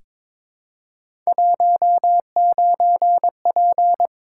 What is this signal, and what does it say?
Morse code, heard as a keyed tone, about 22 wpm, 700 Hz. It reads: 19P
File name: practice / 22wpm / Three Letter-Number Combinations - 22wpm